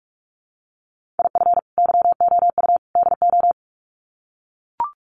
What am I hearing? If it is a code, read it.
Morse code, heard as a keyed tone, about 44 wpm, 700 Hz. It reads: IF YOU DO
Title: Sets of 3 Words 45wpm